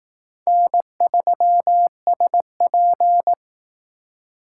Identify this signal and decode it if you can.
Morse code, heard as a keyed tone, about 18 wpm, 700 Hz. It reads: N3SP
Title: Speed Racing - USA Call Signs 15wpm